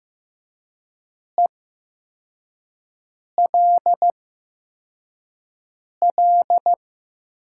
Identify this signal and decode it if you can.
Morse code, heard as a keyed tone, about 15 wpm, spaced as for 4 wpm, 700 Hz. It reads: ELL
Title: 8x Character Spacing - Top 200 Words - 15wpm